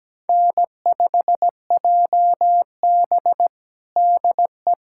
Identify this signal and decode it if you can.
Morse code, heard as a keyed tone, about 17 wpm, 700 Hz. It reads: N5JB DE
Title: USA Callsign DE Callsign - 4 Characters - 17wpm